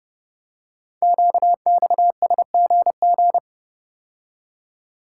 Morse code, heard as a keyed tone, about 30 wpm, 700 Hz. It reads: QXSGG
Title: Code Groups - 5 Letter:Numbers - 30wpm